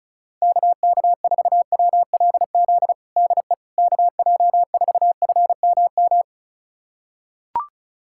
Morse code, heard as a keyed tone, about 36 wpm, 700 Hz. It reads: KK4WLZ DE KJ4FMM
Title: USA Callsign DE Callsign - 6 Characters - 35wpm